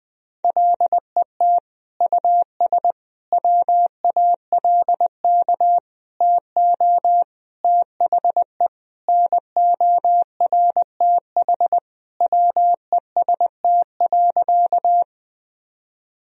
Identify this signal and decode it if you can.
Morse code, heard as a keyed tone, about 20 wpm, 700 Hz. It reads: LET US WALK TO THE NORTH WEST.
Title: Sentences from Top 1000 Words - Review Part 2 - 20wpm